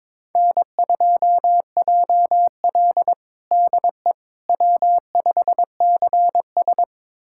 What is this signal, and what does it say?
Morse code, heard as a keyed tone, about 22 wpm, 700 Hz. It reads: N2JL DE W5CS
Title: USA Callsign DE Callsign - 4 Characters - 22wpm